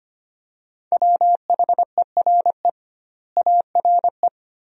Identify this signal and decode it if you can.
Morse code, heard as a keyed tone, about 25 wpm, 700 Hz. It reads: WHERE ARE
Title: 2x Word-Spacing - Sentences from Top 200 Words - 25wpm